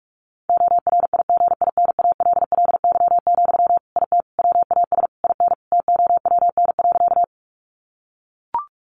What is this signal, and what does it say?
Morse code, heard as a keyed tone, about 44 wpm, 700 Hz. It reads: ORIGINALLY, IT WAS IN TOWN.
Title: Sentences from Top 1500 Words 45wpm